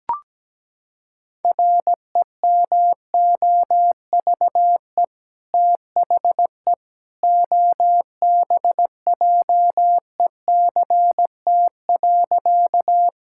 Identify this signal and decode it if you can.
Morse code, heard as a keyed tone, about 17 wpm, 700 Hz. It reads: REMOVE THE OBJECT.
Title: Sentences from Top 1000 Words - Review Part 5 - 17wpm